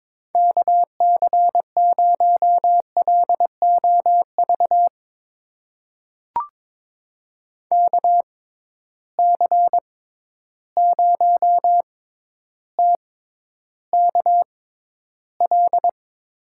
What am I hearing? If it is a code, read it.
Morse code, heard as a keyed tone, about 22 wpm, spaced as for 11 wpm, 700 Hz. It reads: KC0LOV KC0TKL
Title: USA Call Signs - 6 Characters - 6x Character Spacing - 22wpm